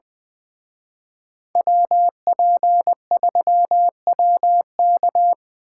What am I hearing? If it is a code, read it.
Morse code, heard as a keyed tone, about 24 wpm, 700 Hz. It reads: WP3WK
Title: Speed Racing - USA 2x2 Call Signs 20wpm